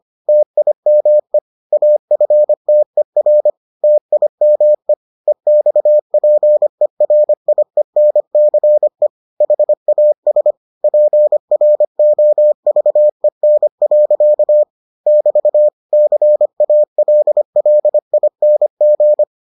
Morse code, heard as a keyed tone, about 25 wpm, 600 Hz. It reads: TIME AFTER TIME EXPERIENCE HAS PROVEN. <BT> CALLING